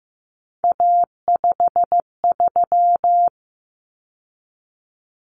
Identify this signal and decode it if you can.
Morse code, heard as a keyed tone, about 15 wpm, 700 Hz. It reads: A53
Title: Common Cut Numbers - 3-Digit 15wpm